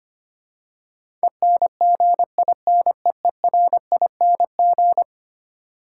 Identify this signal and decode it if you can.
Morse code, heard as a keyed tone, about 25 wpm, 700 Hz. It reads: ENGINEERING
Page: morse-code-ninja-practice-25wpm